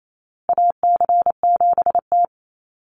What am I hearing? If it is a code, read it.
Morse code, heard as a keyed tone, about 28 wpm, 700 Hz. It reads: AC7T
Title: Call Signs Repeated 2x - 28wpm